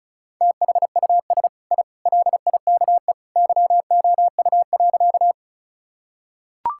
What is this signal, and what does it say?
Morse code, heard as a keyed tone, about 36 wpm, 700 Hz. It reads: THUS I LIKE YOU.
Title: Sentences from Top 1000 Words - Review Part 5 - 35wpm